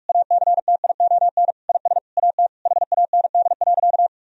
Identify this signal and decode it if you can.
Morse code, heard as a keyed tone, about 46 wpm, 700 Hz. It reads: ACTION IS AT HAND.
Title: Sentences from Top 1000 Words - Review Part 4 - 45wpm